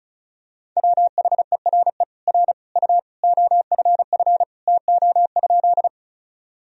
Morse code, heard as a keyed tone, about 34 wpm, 700 Hz. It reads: WHERE R U OFF TO?
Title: Sentences from Top 200 Words and Abbreviations 35wpm